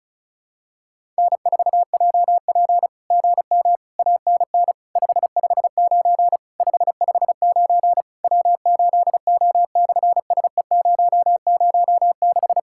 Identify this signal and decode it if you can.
Morse code, heard as a keyed tone, about 36 wpm, 700 Hz. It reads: N4JP GM ANN 559 559 W8O/SE006
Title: SOTA QSO Practice 35wpm